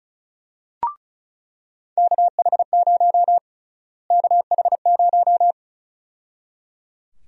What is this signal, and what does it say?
Morse code, heard as a keyed tone, about 34 wpm, 700 Hz. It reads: KH0 KH0
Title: USA Call Sign Prefixes - Repeated 2x - 35wpm